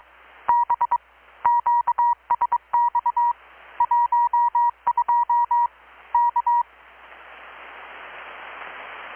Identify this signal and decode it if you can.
Morse code, heard as a keyed tone, about 22 wpm, 950 Hz. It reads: B QSX 12 K
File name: blog / swl / 6-abril-2019-12695-USB-KPS